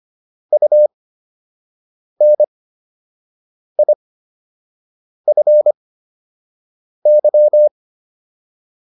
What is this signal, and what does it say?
Morse code, heard as a keyed tone, about 25 wpm, spaced as for 6 wpm, 600 Hz. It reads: UNIFY